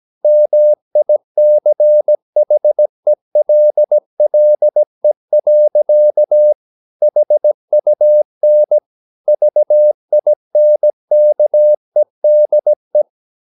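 Morse code, heard as a keyed tone, about 17 wpm, 600 Hz. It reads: MICHELLE. HUN VINKEDE
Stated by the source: Slow (QRS) Morse practice at 17 wpm, from CW med Gnister.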